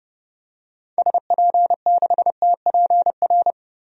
Morse code, heard as a keyed tone, about 30 wpm, 700 Hz. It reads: SP6TPR
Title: DXCC-100 Call Signs - 6 Characters - 30wpm